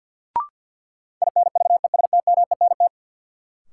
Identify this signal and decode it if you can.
Morse code, heard as a keyed tone, about 53 wpm, 700 Hz. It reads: INVESTMENT
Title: Sentences from Top 700 Words - New Word Drill - 50wpm